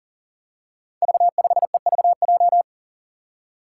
Morse code, heard as a keyed tone, about 40 wpm, 700 Hz. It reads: V5EVJ
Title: Five Challenging Letter-Number Combinations 40wpm